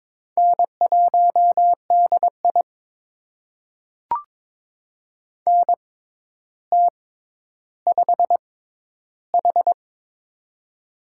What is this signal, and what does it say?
Morse code, heard as a keyed tone, about 22 wpm, spaced as for 11 wpm, 700 Hz. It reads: N1DI NT5H
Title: USA Call Signs - 4 Characters - 6x Character Spacing - 22wpm